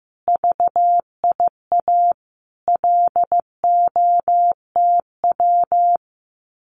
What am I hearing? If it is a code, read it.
Morse code, heard as a keyed tone, about 15 wpm, 700 Hz. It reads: VIA LOTW
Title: Common QSO Phrases Encore 15wpm